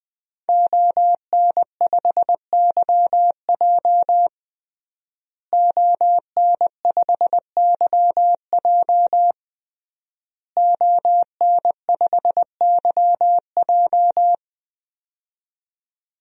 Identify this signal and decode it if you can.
Morse code, heard as a keyed tone, about 20 wpm, 700 Hz. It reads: ON5YJ ON5YJ ON5YJ
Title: DXCC-100 Call Signs - 5 Characters - Repeated 3x - 20wpm